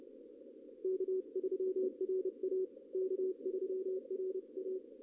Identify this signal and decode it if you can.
Morse code, heard as a keyed tone, about 29 wpm, 380 Hz. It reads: K3RA K3RA
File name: contests / ss-ssb-k5tr- -1625